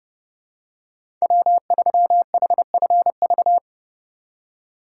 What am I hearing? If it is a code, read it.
Morse code, heard as a keyed tone, about 30 wpm, 700 Hz. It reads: W3HFV
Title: USA Call Signs - 5 Characters - 3x Character Spacing - 30wpm